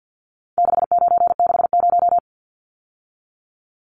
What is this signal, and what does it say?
Morse code, heard as a keyed tone, about 50 wpm, 700 Hz. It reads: 6960
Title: Numbers - 4-Digit 50wpm